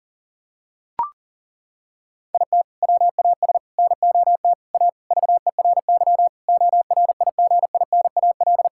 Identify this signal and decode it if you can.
Morse code, heard as a keyed tone, about 40 wpm, 700 Hz. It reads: IT WAS NOT A VERY ORIGINAL
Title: Sentences from Top 1000 Words - Review Part 6 - 40wpm